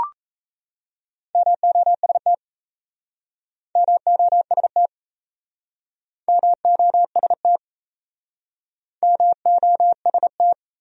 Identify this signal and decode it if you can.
Morse code, heard as a keyed tone, about 38 wpm, 700 Hz. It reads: MOST MOST MOST MOST